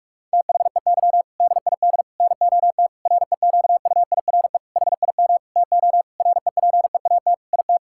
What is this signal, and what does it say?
Morse code, heard as a keyed tone, about 46 wpm, 700 Hz. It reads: THEY DID NOT REQUIRE HIM TO REPEAT IT
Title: Sentences from Top 1600 Words 45wpm